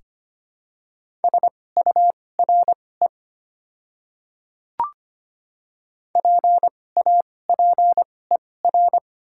Morse code, heard as a keyed tone, about 25 wpm, spaced as for 18 wpm, 700 Hz. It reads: SURE PAPER
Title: 2x Character Spacing - Top 300 Words - 25wpm